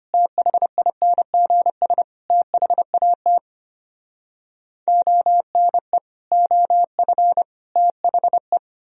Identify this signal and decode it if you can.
Morse code, heard as a keyed tone, about 33 wpm, 700 Hz. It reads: THINGS THAT ONE OF THE